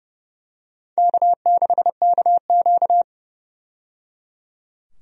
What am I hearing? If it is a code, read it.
Morse code, heard as a keyed tone, about 30 wpm, 700 Hz. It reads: K6KQ
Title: USA Call Signs - 4 Characters - 30wpm